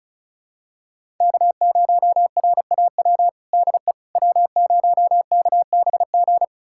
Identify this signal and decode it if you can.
Morse code, heard as a keyed tone, about 34 wpm, 700 Hz. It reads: K0RAW DE W0KBG
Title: USA Callsign DE Callsign - 5 Characters - 35wpm